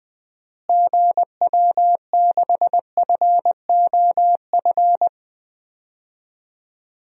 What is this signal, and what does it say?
Morse code, heard as a keyed tone, about 20 wpm, 690 Hz. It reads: GW6FOF
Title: DXCC-100 Call Signs - 6 Characters - Repeated 2x - 20wpm